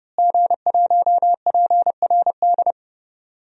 Morse code, heard as a keyed tone, about 30 wpm, 700 Hz. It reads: G1PRD
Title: DXCC-100 Call Signs - 5 Characters - Repeated 2x - 30wpm